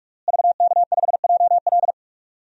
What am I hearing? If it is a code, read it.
Morse code, heard as a keyed tone, about 47 wpm, 700 Hz. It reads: VK5JL
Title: DXCC-100 Call Signs - 5 Characters - Repeated 3x - 45wpm